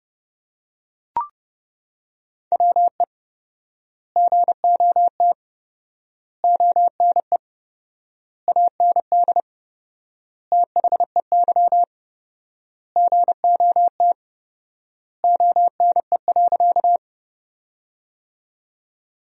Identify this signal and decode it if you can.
Morse code, heard as a keyed tone, about 31 wpm, 700 Hz. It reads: WE GOT ONE AND THEY GOT ONE.
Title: 4x Word-Spacing - Sentences from Top 300 Words - 30wpm